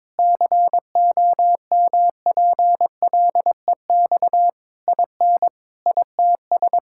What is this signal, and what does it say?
Morse code, heard as a keyed tone, about 22 wpm, 700 Hz. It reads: COMPLEX IN ITS
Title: Sentences from Top 1000 Words - Review Part 6 - 22wpm